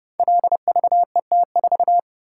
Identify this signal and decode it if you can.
Morse code, heard as a keyed tone, about 28 wpm, 700 Hz. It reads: LVET4
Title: Code Groups - 5 Letter:Numbers - 30wpm